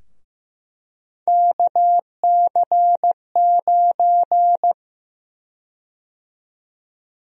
Morse code, heard as a keyed tone, about 15 wpm, 700 Hz. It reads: KC9
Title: USA Call Sign Prefixes - Repeated 2x - 15wpm